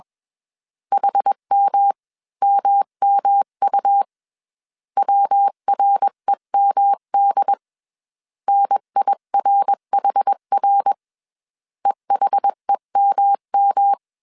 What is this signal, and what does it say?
Morse code, heard as a keyed tone, about 21 wpm, 800 Hz. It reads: HMMMU WREMD NIRHR EHEMM
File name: CW / DET / 1 3 1500C S B 20wpm 1754